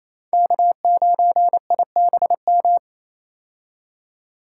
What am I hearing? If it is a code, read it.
Morse code, heard as a keyed tone, about 27 wpm, 700 Hz. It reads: K9IBM
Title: K1USN SST QSO Practice - 28wpm